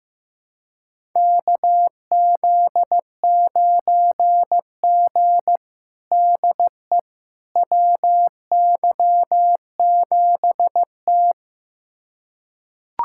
Morse code, heard as a keyed tone, about 15 wpm, 700 Hz. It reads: KZ9G DE WY7T
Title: USA Callsign DE Callsign - 4 Characters - 15wpm